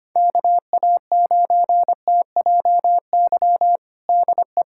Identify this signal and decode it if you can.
Morse code, heard as a keyed tone, about 25 wpm, 700 Hz. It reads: KA9TJY DE
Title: USA Callsign DE Callsign 25wpm